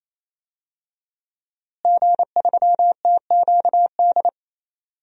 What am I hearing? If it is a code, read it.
Morse code, heard as a keyed tone, about 28 wpm, 700 Hz. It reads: G3TQD
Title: DXCC-100 Call Signs - 5 Characters - Repeated 3x - 28wpm